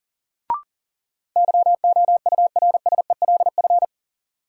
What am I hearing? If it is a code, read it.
Morse code, heard as a keyed tone, about 40 wpm, 700 Hz. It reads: YOURSELF
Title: Sentences from Top 600 Words - New Word Drill - 40wpm